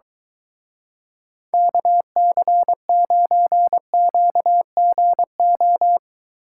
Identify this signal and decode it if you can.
Morse code, heard as a keyed tone, about 20 wpm, 700 Hz. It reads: KC9QGO
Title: Speed Racing - USA 2x3 Call Signs 17wpm